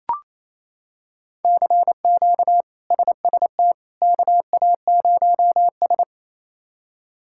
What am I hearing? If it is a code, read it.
Morse code, heard as a keyed tone, about 27 wpm, 700 Hz. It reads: CQ SST KA0S
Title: K1USN SST QSO Practice - 28wpm